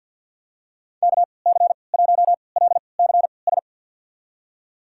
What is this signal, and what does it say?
Morse code, heard as a keyed tone, about 52 wpm, spaced as for 28 wpm, 700 Hz. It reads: KC1LCS
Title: USA Call Signs - 6 Characters - 3x Character Spacing - 50wpm